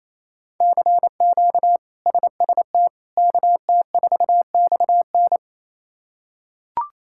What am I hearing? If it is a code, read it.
Morse code, heard as a keyed tone, about 28 wpm, 700 Hz. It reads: CQ SST KT4XN
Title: K1USN SST QSO Practice - 28wpm